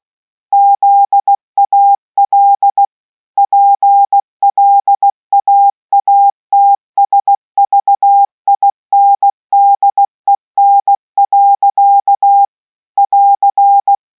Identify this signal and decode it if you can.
Morse code, heard as a keyed tone, about 16 wpm, 800 Hz. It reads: ZAL PLAATSVINDEN. <AR>